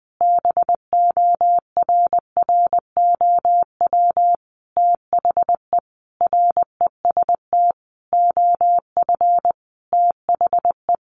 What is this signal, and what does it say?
Morse code, heard as a keyed tone, about 20 wpm, 700 Hz. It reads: BORROW THE REST OF THE
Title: Sentences from Top 1500 Words 20wpm